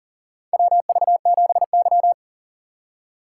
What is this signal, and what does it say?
Morse code, heard as a keyed tone, about 39 wpm, 700 Hz. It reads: WV7Y
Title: USA Call Signs - 4 Characters - 3x Character Spacing - 40wpm